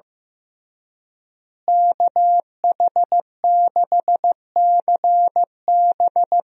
Morse code, heard as a keyed tone, about 16 wpm, 700 Hz. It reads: KH6CB
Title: USA Call Signs - 5 Characters - Repeated 2x - 15wpm